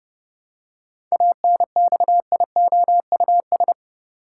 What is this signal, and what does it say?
Morse code, heard as a keyed tone, about 30 wpm, 700 Hz. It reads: ANXIOUS
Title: Seven Letter Words - 30wpm